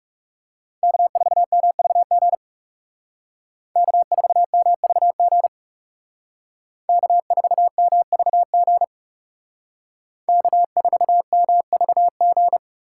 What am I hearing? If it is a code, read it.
Morse code, heard as a keyed tone, about 37 wpm, 700 Hz. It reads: K4MVG K4MVG K4MVG K4MVG
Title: Speed Racing - USA Call Signs 30wpm